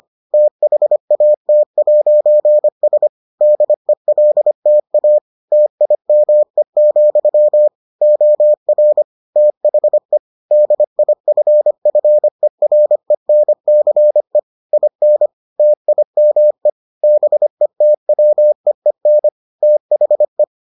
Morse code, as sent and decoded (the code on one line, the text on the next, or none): - .... .- - .----. ... / -.. .-.. - .- / - .. -- . --..-- / --- .-. / - .... / -.. .. ..-. ..-. .-. -. -.-. / .. -. / - .. -- . / -... - .-- . -. / - .... .
THAT'S DELTA TIME, OR THE DIFFERENCE IN TIME BETWEEN THE